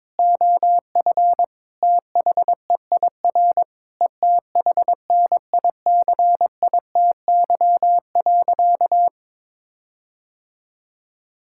Morse code, heard as a keyed tone, about 22 wpm, 700 Hz. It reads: OF THEIR ETHNICITY.
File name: Sentences with 9-Word Phrases - 22wpm